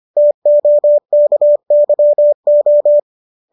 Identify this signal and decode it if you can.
Morse code, heard as a keyed tone, about 25 wpm, 590 Hz. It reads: TOKYO
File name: qso205 25